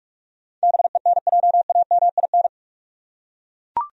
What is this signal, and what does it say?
Morse code, heard as a keyed tone, about 41 wpm, 700 Hz. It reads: BENJAMIN
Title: USA Common Names - 45wpm